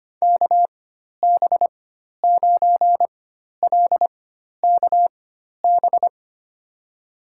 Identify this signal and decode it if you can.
Morse code, heard as a keyed tone, about 25 wpm, spaced as for 12 wpm, 700 Hz. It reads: KB9LKB
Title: USA Call Signs - 6 Characters - 4x Character Spacing - 25wpm